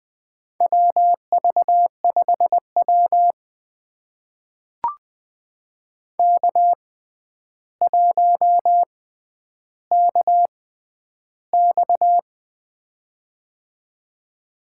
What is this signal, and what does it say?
Morse code, heard as a keyed tone, about 20 wpm, spaced as for 10 wpm, 700 Hz. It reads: WV5W K1KX
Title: USA Call Signs - 4 Characters - 6x Character Spacing - 20wpm